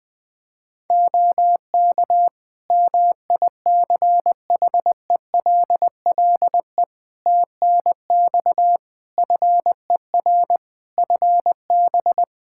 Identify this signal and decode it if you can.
Morse code, heard as a keyed tone, about 20 wpm, 700 Hz. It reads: OK MICHELLE TNX FER FB
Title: Practice QSO - Exchange 4 - Ending - 20wpm